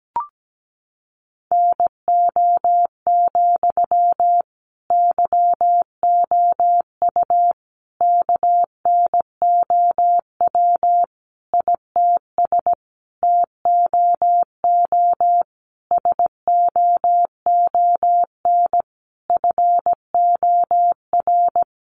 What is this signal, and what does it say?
Morse code, heard as a keyed tone, about 17 wpm, 700 Hz. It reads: NO, YOU KNOW ITS TOO SOON FOR
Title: Sentences from Top 800 Words 17wpm